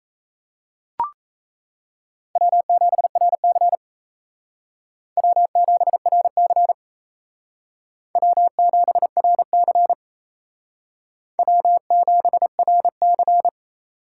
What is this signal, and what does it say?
Morse code, heard as a keyed tone, about 36 wpm, 700 Hz. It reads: W7RC W7RC W7RC W7RC